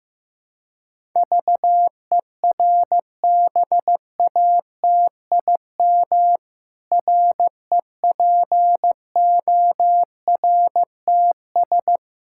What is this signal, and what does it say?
Morse code, heard as a keyed tone, about 15 wpm, 700 Hz. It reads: VERBATIM REPORTS
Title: Pangram Challenge v2 - 15wpm